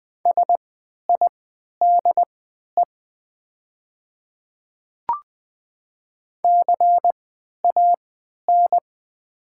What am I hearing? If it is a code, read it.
Morse code, heard as a keyed tone, about 20 wpm, spaced as for 11 wpm, 700 Hz. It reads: SIDE CAN
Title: 3x Character Spacing - Top 100 Words - 20wpm